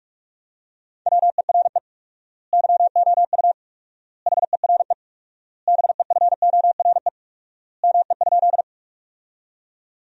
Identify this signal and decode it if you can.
Morse code, heard as a keyed tone, about 44 wpm, 700 Hz. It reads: WERE YOU HERE BEFORE ME?
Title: 4x Word-Spacing - Sentences from Top 200 Words - 45wpm